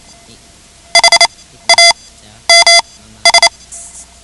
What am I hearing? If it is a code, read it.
Morse code, heard as a keyed tone, about 28 wpm, spaced as for 13 wpm, 2050 Hz. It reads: HAMS
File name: ham radio / CodeQuick / Code Quick 10